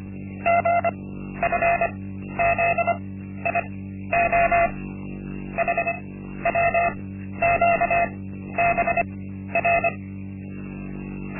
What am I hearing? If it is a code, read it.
Morse code, heard as a keyed tone, about 24 wpm, spaced as for 12 wpm, 680 Hz. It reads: GFZIO HWQBR
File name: Linksmakalnis-1